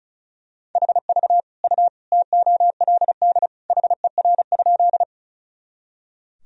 Morse code, heard as a keyed tone, about 34 wpm, 700 Hz. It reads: HV U TOLD HER?
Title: Sentences from Top 400 Words and Abbreviations 35wpm